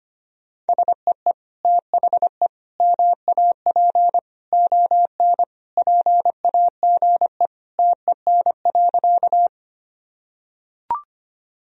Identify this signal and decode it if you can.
Morse code, heard as a keyed tone, about 25 wpm, 700 Hz. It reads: SEE THE MAP ON PAGE TEN.
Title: Sentences from Top 1000 Words - Review Part 2 - 25wpm